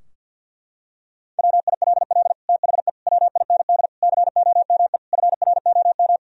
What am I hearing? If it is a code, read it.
Morse code, heard as a keyed tone, about 51 wpm, 700 Hz. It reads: WILL THE WIND COME FROM
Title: Sentences from Top 1000 Words - Review Part 2 - 50wpm